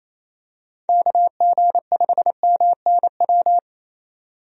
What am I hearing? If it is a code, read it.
Morse code, heard as a keyed tone, about 28 wpm, 700 Hz. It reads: KG5MNW
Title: USA Call Signs - 6 Characters - 28wpm